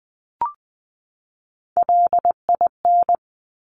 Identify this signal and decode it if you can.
Morse code, heard as a keyed tone, about 20 wpm, 700 Hz. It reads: LIN
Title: Top Three Letter Combinations in Words 20wpm